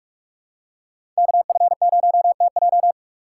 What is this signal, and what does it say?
Morse code, heard as a keyed tone, about 46 wpm, 700 Hz. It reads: KF0TJ
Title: USA Call Signs - 5 Characters - 2x Character Spacing - 45wpm